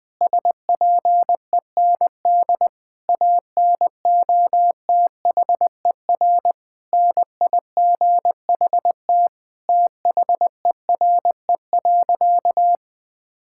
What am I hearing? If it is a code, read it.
Morse code, heard as a keyed tone, about 20 wpm, 700 Hz. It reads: SPEND ANOTHER NIGHT THERE.
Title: Sentences from Top 1000 Words - Review Part 5 - 20wpm